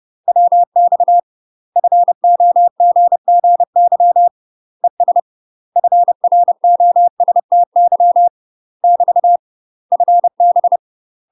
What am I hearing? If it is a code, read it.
Morse code, heard as a keyed tone, about 30 wpm, 700 Hz. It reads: WX FOGGY ES FROSTY <BT> FB
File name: UK MASTER QSOs TRAINING/30WPM